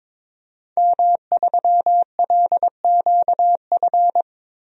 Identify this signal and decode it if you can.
Morse code, heard as a keyed tone, about 22 wpm, 700 Hz. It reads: M3LQF
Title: DXCC-100 Call Signs - 5 Characters - Repeated 2x - 22wpm